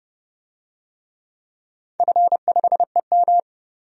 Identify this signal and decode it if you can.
Morse code, heard as a keyed tone, about 32 wpm, 700 Hz. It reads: F5EM
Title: DXCC-100 Call Signs - 4 Characters - 30wpm